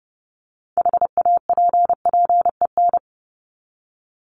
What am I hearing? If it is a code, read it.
Morse code, heard as a keyed tone, about 30 wpm, 700 Hz. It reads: HAPPEN
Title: ED Suffix - Top 100 Words with Base Words 30wpm